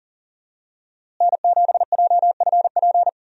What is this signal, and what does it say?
Morse code, heard as a keyed tone, about 40 wpm, 700 Hz. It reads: N7JFP
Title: USA Call Signs - 5 Characters - Repeated 3x - 40wpm